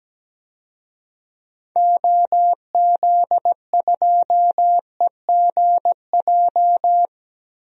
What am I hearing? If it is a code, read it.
Morse code, heard as a keyed tone, about 17 wpm, 700 Hz. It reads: OZ2EGJ
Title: DXCC-100 Call Signs - 6 Characters - Repeated 2x - 17wpm